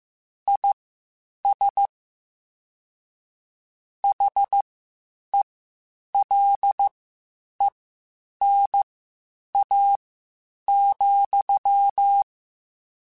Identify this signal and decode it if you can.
Morse code, heard as a keyed tone, about 15 wpm, spaced as for 8 wpm, 800 Hz. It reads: IS HELENA,
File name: (spaced) / QSOs Spaced-mp3s / WD0BGP 1 @13 sp